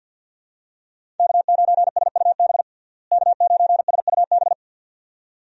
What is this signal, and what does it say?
Morse code, heard as a keyed tone, about 51 wpm, 700 Hz. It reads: K9SUB K9SUB
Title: USA Call Signs - 5 Characters - Repeated 2x - 50wpm